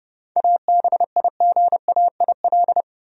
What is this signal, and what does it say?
Morse code, heard as a keyed tone, about 30 wpm, 700 Hz. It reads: ABIGAIL
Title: USA Common Names - 30wpm